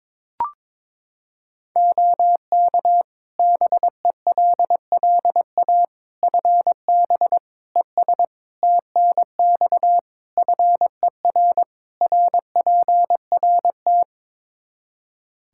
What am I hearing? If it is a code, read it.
Morse code, heard as a keyed tone, about 22 wpm, 700 Hz. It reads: OK BELLA FB ES TNX FER RPRT
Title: K1USN SST QSO Practice 22wpm